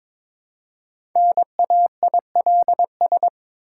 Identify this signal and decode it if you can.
Morse code, heard as a keyed tone, about 22 wpm, 690 Hz. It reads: NAILS
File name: Difficult Words with Hints - 22wpm